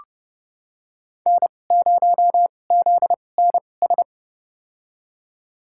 Morse code, heard as a keyed tone, about 31 wpm, spaced as for 22 wpm, 700 Hz. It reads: N0ZNS
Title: USA Call Signs - 5 Characters - 2x Character Spacing - 30wpm